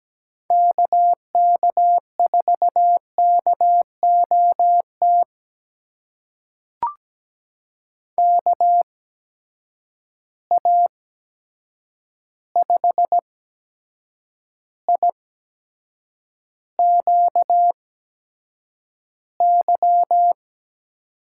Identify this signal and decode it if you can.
Morse code, heard as a keyed tone, about 17 wpm, spaced as for 7 wpm, 700 Hz. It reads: KK4KOT KA5IQY
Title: USA Call Signs - 6 Characters - 8x Character Spacing - 17wpm